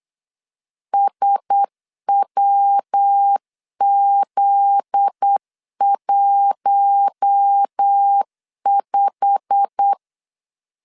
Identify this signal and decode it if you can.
Morse code, heard as a keyed tone, about 8 wpm, 800 Hz. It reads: SWZ15
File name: LEZIONI / 9 2 8wpm 3437